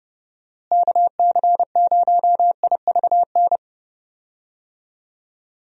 Morse code, heard as a keyed tone, about 29 wpm, 700 Hz. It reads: KC0IVN
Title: USA Call Signs - 6 Characters - Repeated 2x - 30wpm